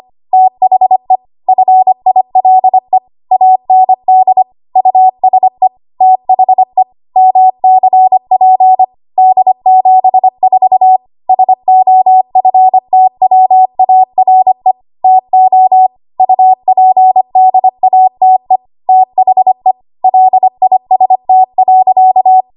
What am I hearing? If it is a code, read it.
Morse code, heard as a keyed tone, about 25 wpm, 750 Hz. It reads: THE FILE AND USE THE MCP D74 SOFTWARE TO UPDATE THE LIST.
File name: Archive / 25 WPM / 25WPM